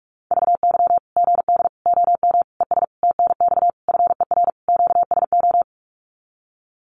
Morse code, heard as a keyed tone, about 44 wpm, 700 Hz. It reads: VY GD OM ES TNX FER QSO
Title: Common QSO Phrases 45wpm